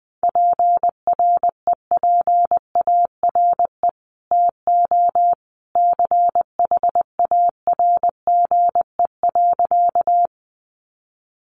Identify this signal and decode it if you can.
Morse code, heard as a keyed tone, about 20 wpm, 700 Hz. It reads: PREPARE TO CHARGE.
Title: Sentences from Top 900 Words 20wpm